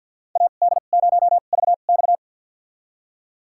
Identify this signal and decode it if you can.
Morse code, heard as a keyed tone, about 46 wpm, spaced as for 35 wpm, 700 Hz. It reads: AD0VX
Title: USA Call Signs - 5 Characters - 2x Character Spacing - 50wpm